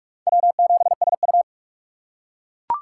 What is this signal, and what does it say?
Morse code, heard as a keyed tone, about 40 wpm, 700 Hz. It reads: W7SU